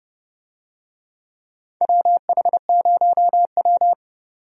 Morse code, heard as a keyed tone, about 30 wpm, 700 Hz. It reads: WH0W
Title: USA Call Signs - 4 Characters - 30wpm